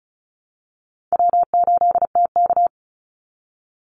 Morse code, heard as a keyed tone, about 34 wpm, 700 Hz. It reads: W8TK
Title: Easy Call Signs Repeated 2x - 35wpm